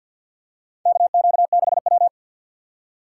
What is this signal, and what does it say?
Morse code, heard as a keyed tone, about 53 wpm, 700 Hz. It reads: KQ6W
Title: USA Call Signs - 4 Characters - 2x Character Spacing - 50wpm